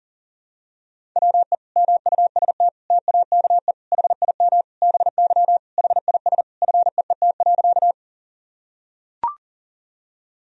Morse code, heard as a keyed tone, about 40 wpm, 700 Hz. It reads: WE MUST TAKE HIM BY HIS FEET.
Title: Sentences from Top 1000 Words - Review Part 1 - 40wpm